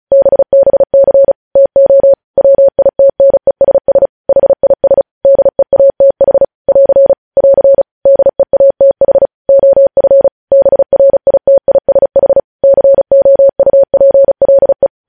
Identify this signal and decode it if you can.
Morse code, heard as a keyed tone, about 36 wpm, 550 Hz. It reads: BBC TO WITNESS HIS DEATH <AR> <AR> DEATH OF BRITISH COUPLE
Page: news headlines in morse code at 35 wpm